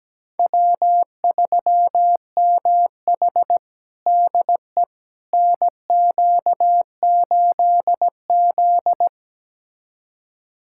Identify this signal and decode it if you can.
Morse code, heard as a keyed tone, about 17 wpm, 700 Hz. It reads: W3MH DE NQ8Z
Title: USA Callsign DE Callsign - 4 Characters - 17wpm